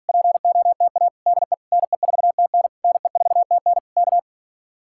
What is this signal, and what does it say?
Morse code, heard as a keyed tone, about 52 wpm, 700 Hz. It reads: POTA DE NE4TN NE4TN K